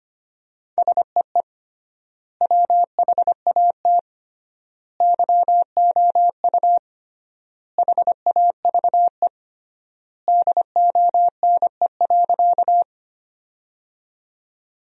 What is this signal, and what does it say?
Morse code, heard as a keyed tone, about 25 wpm, 700 Hz. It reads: SEE WHAT YOU HAVE DONE.
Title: 3x Word-Spacing - Sentences from Top 500 Words - 25wpm